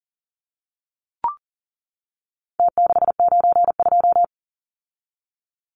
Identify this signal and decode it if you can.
Morse code, heard as a keyed tone, about 40 wpm, 700 Hz. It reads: T692
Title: Numbers - Contest Style - 4-Digit 40wpm